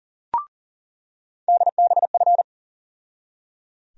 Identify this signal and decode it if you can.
Morse code, heard as a keyed tone, about 38 wpm, 700 Hz. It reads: DBF
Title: USA Call Sign Postfixes - 3 Letters 40wpm